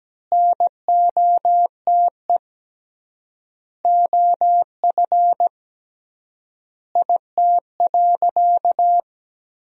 Morse code, heard as a keyed tone, about 17 wpm, 700 Hz. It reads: NOTE OF IT.
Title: 3x Word-Spacing - Sentences from Top 500 Words - 17wpm